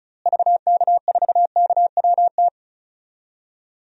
Morse code, heard as a keyed tone, about 36 wpm, 700 Hz. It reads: VK4KWT
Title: DXCC-100 Call Signs - 6 Characters - Repeated 2x - 35wpm